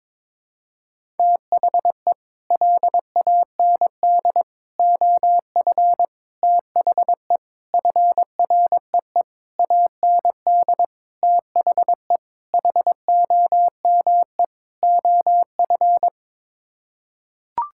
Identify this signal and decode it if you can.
Morse code, heard as a keyed tone, about 22 wpm, 700 Hz. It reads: THE LAND OF THE FREE AND THE HOME OF
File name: Sets of 9 Words - 22wpm